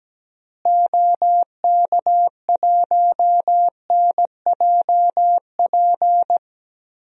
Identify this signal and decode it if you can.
Morse code, heard as a keyed tone, about 17 wpm, 700 Hz. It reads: OK1NJP
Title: DXCC-100 Call Signs - 6 Characters - Repeated 3x - 17wpm